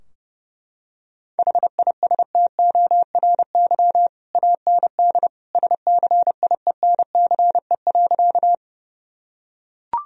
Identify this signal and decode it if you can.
Morse code, heard as a keyed tone, about 30 wpm, 700 Hz. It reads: HISTORY AND SCIENCE.
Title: Sentences from Top 1000 Words - Review Part 2 - 30wpm